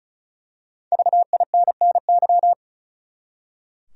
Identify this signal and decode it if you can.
Morse code, heard as a keyed tone, about 34 wpm, 700 Hz. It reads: VINNY
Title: USA Common Names - 35wpm